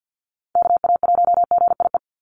Morse code, heard as a keyed tone, about 50 wpm, 700 Hz. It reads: KA1GIE
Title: USA Call Signs 50wpm